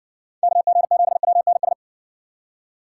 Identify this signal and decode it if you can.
Morse code, heard as a keyed tone, about 56 wpm, 700 Hz. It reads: KG7WNS